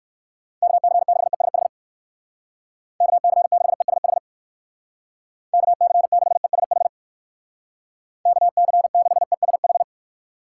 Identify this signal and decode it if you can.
Morse code, heard as a keyed tone, about 56 wpm, 700 Hz. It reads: KC6ESH KC6ESH KC6ESH KC6ESH
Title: Speed Racing - USA Call Signs 45wpm